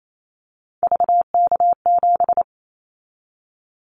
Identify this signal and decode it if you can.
Morse code, heard as a keyed tone, about 28 wpm, 700 Hz. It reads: VK7
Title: Callsign Prefixes for Top 50 DXCC - 28wpm